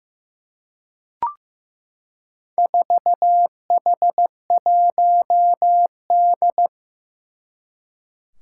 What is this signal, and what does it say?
Morse code, heard as a keyed tone, about 15 wpm, 700 Hz. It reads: 4H1D
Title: Four Challenging Letter-Number Combinations 15wpm